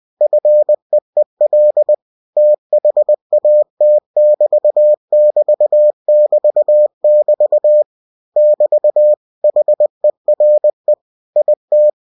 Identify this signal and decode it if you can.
Morse code, heard as a keyed tone, about 20 wpm, 600 Hz. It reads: FEEL THAT <BT><BT><BT><BT> <BT> HERE IT
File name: cw-books / winnie-the-pooh / winnie-20wpm-page-0108